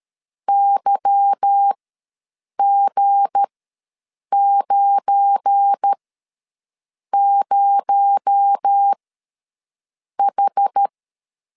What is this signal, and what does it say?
Morse code, heard as a keyed tone, about 13 wpm, spaced as for 6 wpm, 800 Hz. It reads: YG90H
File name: LEZIONI / 8 YJ45 / 8 Y ws 12wpm 3409